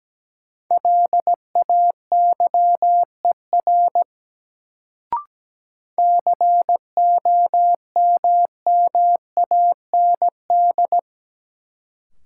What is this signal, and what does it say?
Morse code, heard as a keyed tone, about 17 wpm, 700 Hz. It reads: LAYER COMMAND
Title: Sentences from Top 1600 Words - New Word Drill Alt - 17wpm